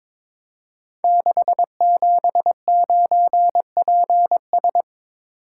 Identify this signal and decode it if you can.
Morse code, heard as a keyed tone, about 22 wpm, 700 Hz. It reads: 679PS
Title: Code Groups - 5 Letter:Numbers - 22wpm